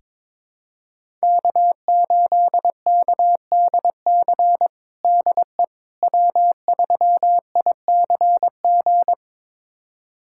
Morse code, heard as a keyed tone, about 21 wpm, 700 Hz. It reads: K8KDC DE W3ICG
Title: USA Callsign DE Callsign - 5 Characters - 22wpm